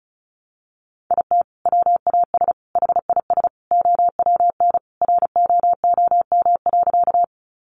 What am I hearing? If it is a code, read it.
Morse code, heard as a keyed tone, about 34 wpm, 700 Hz. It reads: IT WAS HIS OWN ROOM.
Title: Sentences from Top 300 Words 35wpm